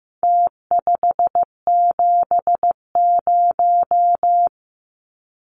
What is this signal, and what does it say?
Morse code, heard as a keyed tone, about 15 wpm, 700 Hz. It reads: T570
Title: Numbers - Contest Style - 4-Digit 15wpm